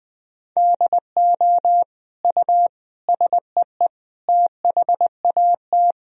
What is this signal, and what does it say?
Morse code, heard as a keyed tone, about 21 wpm, 700 Hz. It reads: DO U SEE THAT
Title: Sentences from Top 500 Words and Abbreviations 20wpm